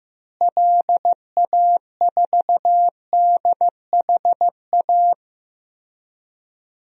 Morse code, heard as a keyed tone, about 16 wpm, 700 Hz. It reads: LA4DHA
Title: DXCC-100 Call Signs - 6 Characters - Repeated 3x - 15wpm